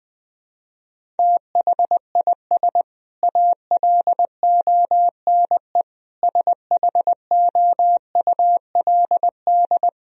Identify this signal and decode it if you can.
Morse code, heard as a keyed tone, about 20 wpm, 700 Hz. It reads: THIS ALONE SHOULD
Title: Sentences from Top 1000 Words - Review Part 4 - 20wpm